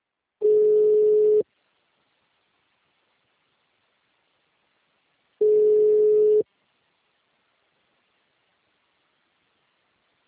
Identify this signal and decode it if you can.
Ringback tone (Europe, 425 Hz).
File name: out- -2034- - - .2